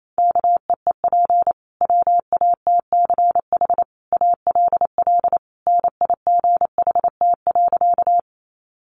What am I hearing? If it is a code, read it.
Morse code, heard as a keyed tone, about 28 wpm, 700 Hz. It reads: KEEP WATCH ALL NIGHT.
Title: Sentences from Top 300 Words 28wpm